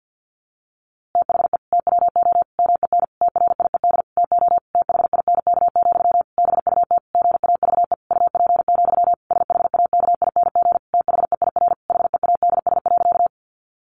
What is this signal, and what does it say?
Morse code, heard as a keyed tone, about 50 wpm, 700 Hz. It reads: THE TWO MEN TRIED TO THINK, BUT GAVE UP, SHAKING THEIR HEADS.
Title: A Sound Of Thunder 50wpm